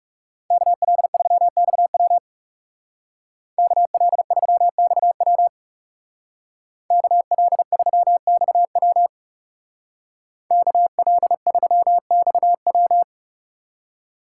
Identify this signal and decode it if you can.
Morse code, heard as a keyed tone, about 36 wpm, 700 Hz. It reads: KL3XW KL3XW KL3XW KL3XW
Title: Speed Racing - USA Call Signs 30wpm